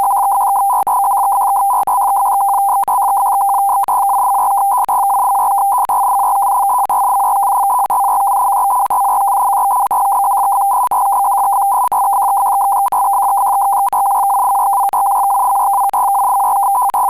Two-tone teleprinter signal, about 200 Hz shift.
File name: pactor